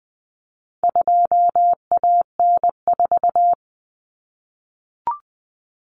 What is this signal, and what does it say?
Morse code, heard as a keyed tone, about 20 wpm, 700 Hz. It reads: 2AN4
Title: Common Cut Numbers - 4-Digit 20wpm